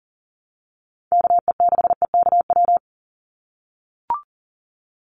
Morse code, heard as a keyed tone, about 40 wpm, 700 Hz. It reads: KE6EKW
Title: USA Call Signs 40wpm